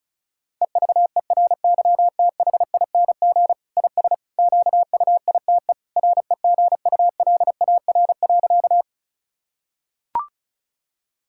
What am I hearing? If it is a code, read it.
Morse code, heard as a keyed tone, about 36 wpm, 700 Hz. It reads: EVERYTHING IS QUITE REGULAR.
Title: Sentences from Top 1000 Words - Review Part 6 - 35wpm